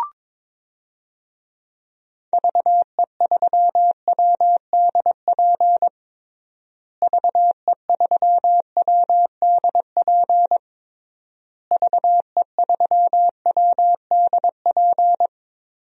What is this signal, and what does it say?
Morse code, heard as a keyed tone, about 22 wpm, 700 Hz. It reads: VE3WDP VE3WDP VE3WDP
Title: DXCC-100 Call Signs - 6 Characters - Repeated 3x - 22wpm